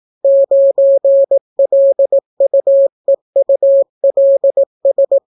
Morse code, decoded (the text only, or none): 9LUEULS